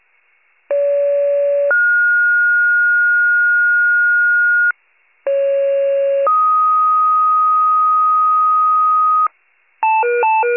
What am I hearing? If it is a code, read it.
Two-tone page, heard as a one-second tone, then a three-second tone, about 570 then 1430 Hz, about 550 then 1250 Hz, with noise. It is heard from 0.7 to 9.3 s.
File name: pager / 28 FIRE / wcalert